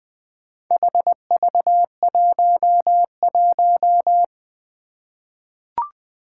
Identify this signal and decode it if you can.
Morse code, heard as a keyed tone, about 20 wpm, 700 Hz. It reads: HV11
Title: Four Challenging Letter-Number Combinations 20wpm